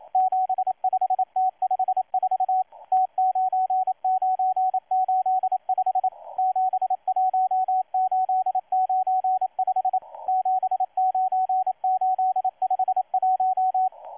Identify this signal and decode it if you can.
Morse code, heard as a keyed tone, about 27 wpm, 750 Hz, with radio noise. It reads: 75T54 T9985 71895 79851
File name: M12-2016-07-07-1310utc-13926khz-error-restart